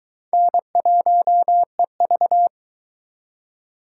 Morse code, heard as a keyed tone, about 23 wpm, 700 Hz. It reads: N1EV